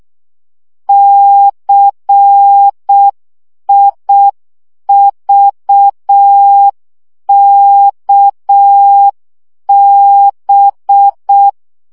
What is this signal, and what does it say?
Morse code, heard as a keyed tone, about 6 wpm, 800 Hz. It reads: CIVKB
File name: morze vizsga 2011-3 5